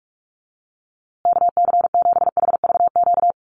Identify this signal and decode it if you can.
Morse code, heard as a keyed tone, about 44 wpm, 700 Hz. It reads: KC7HVQ
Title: USA Call Signs Repeated 3x 45wpm